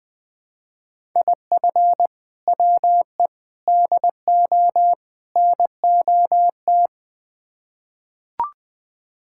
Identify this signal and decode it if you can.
Morse code, heard as a keyed tone, about 20 wpm, 700 Hz. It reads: IF WE DO NOT
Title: Build A Phrase - 4-Word Phrases - 20wpm